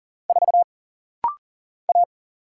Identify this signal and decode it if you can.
Morse code, heard as a keyed tone, about 40 wpm, 700 Hz. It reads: 4 A
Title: Single Letter-Number Alt - Rapid-Fire 40wpm